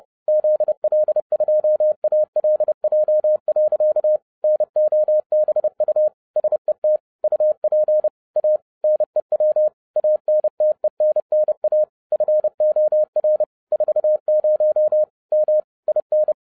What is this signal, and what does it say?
Morse code, heard as a keyed tone, about 30 wpm, 610 Hz. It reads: ZL2ALJ. NOBU SET UP A NEW ANTENNA FOR 40 M IN
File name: 30wpm